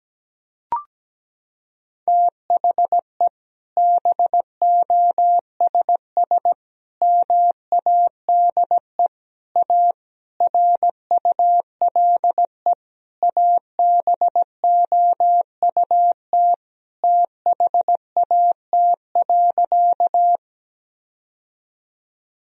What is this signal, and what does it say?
Morse code, heard as a keyed tone, about 17 wpm, 700 Hz. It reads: THE BOSS MADE A RULE ABOUT THAT.
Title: Sentences from Top 1000 Words - Review Part 4 - 17wpm